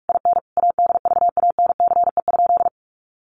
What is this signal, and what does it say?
Morse code, heard as a keyed tone, about 44 wpm, 700 Hz. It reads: IN ADVANCE?
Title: Sentences from Top 1400 Words 45wpm